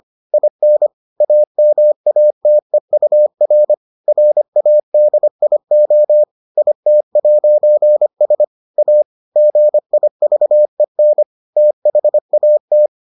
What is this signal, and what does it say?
Morse code, heard as a keyed tone, about 25 wpm, 600 Hz. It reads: IN AMATEUR RADIO IT'S A GIVEN THAT